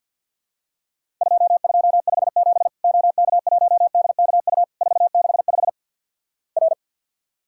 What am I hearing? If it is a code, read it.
Morse code, heard as a keyed tone, about 50 wpm, 700 Hz. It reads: 2257 OK1DKU 465 R
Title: Worked All Europe DX Contest - QTC Exchange with Repeats - 50wpm